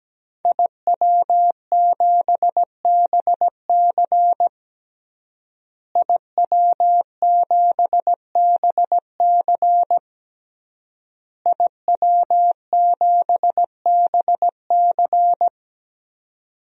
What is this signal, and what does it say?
Morse code, heard as a keyed tone, about 17 wpm, 700 Hz. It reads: IW7BC IW7BC IW7BC
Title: DXCC-100 Call Signs - 5 Characters - Repeated 3x - 17wpm